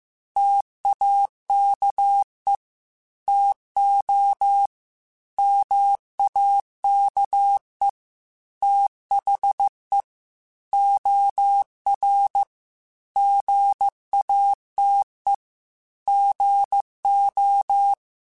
Morse code, heard as a keyed tone, about 15 wpm, 800 Hz. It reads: TAKE TO MAKE THE OR GATE GO
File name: Technical Subjects / Technical Subjects-mp3s / OR Gate @13